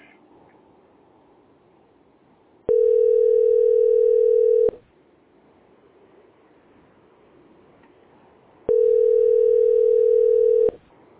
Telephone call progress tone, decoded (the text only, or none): ringback tone